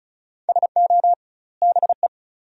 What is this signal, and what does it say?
Morse code, heard as a keyed tone, about 34 wpm, 700 Hz. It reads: SO BE
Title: 2x Word-Spacing - Sentences from Top 100 Words - 35wpm